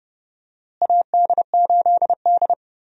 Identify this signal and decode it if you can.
Morse code, heard as a keyed tone, about 31 wpm, 700 Hz. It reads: AD8D
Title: USA Call Signs - 4 Characters - Repeated 2x - 30wpm